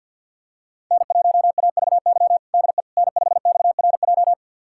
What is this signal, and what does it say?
Morse code, heard as a keyed tone, about 47 wpm, 700 Hz. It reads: N1AVY DE N5XRJ
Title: USA Callsign DE Callsign - 5 Characters - 50wpm